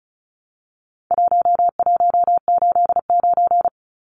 Morse code, heard as a keyed tone, about 34 wpm, 700 Hz. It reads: 1189
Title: Numbers - 4-Digit 35wpm